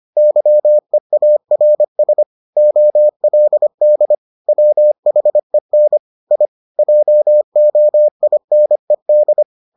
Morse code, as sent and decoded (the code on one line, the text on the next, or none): -.-- . .- .-. ... / --- .-.. -.. / .-- .... -. / .. / .--- --- .. -. -..
YEARS OLD WHEN I JOINED